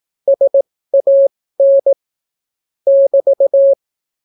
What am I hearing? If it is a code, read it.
Morse code, heard as a keyed tone, about 17 wpm, spaced as for 14 wpm, 550 Hz. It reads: SAN <BT>